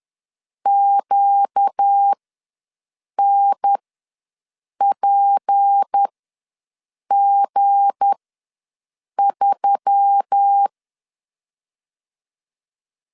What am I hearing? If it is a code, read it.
Morse code, heard as a keyed tone, about 10 wpm, spaced as for 6 wpm, 800 Hz. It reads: QNPG3
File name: LEZIONI / 7 GF9Z / 7 G ws 10wpm 3843